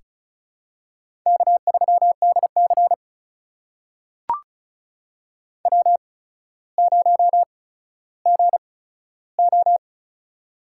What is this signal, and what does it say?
Morse code, heard as a keyed tone, about 36 wpm, spaced as for 15 wpm, 700 Hz. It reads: K3DC W0GO
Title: USA Call Signs - 4 Characters - 8x Character Spacing - 35wpm